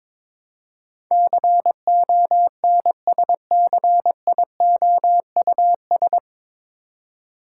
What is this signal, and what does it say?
Morse code, heard as a keyed tone, about 21 wpm, 700 Hz. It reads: CONSCIOUS
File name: 9 Letter Words - First 3 Letters Given - 22wpm